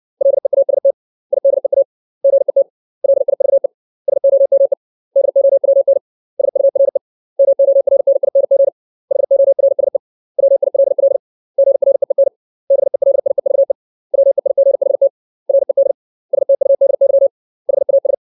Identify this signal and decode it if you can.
Morse code, heard as a keyed tone, about 60 wpm, 550 Hz. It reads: LEAST IDEA GET LIVE SOME DOWN SURE MORNING HORSE WILL GREEN BELIEVE WEIGHT RED STUDY HAS
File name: 632 words / 632 Words 60WPM